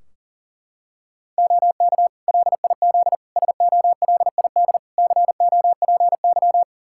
Morse code, heard as a keyed tone, about 40 wpm, 700 Hz. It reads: OK LIZ SOLID COPY
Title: K1USN SST QSO Practice 40wpm